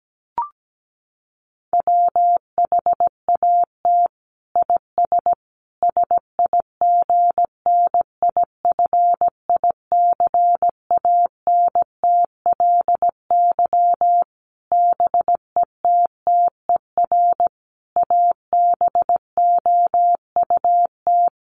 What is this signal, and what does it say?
Morse code, heard as a keyed tone, about 17 wpm, 700 Hz. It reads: WHAT IS SIGNIFICANTLY BETTER ABOUT
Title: Sentences from Top 1500 Words 17wpm